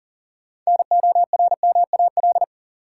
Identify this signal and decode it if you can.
Morse code, heard as a keyed tone, about 40 wpm, 700 Hz. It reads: NORMAL
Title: Sentences from Top 1200 Words - New Word Drill Alt - 40wpm